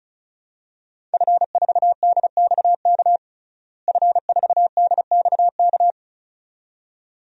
Morse code, heard as a keyed tone, about 34 wpm, 700 Hz. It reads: F4DXK F4DXK
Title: DXCC-100 Call Signs - 5 Characters - Repeated 2x - 35wpm